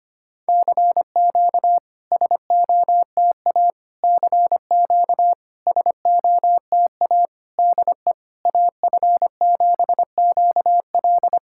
Morse code, heard as a keyed tone, about 25 wpm, 690 Hz. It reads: CQ SOTA CQ SOTA DE AF7QL
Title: SOTA QSO Practice 25wpm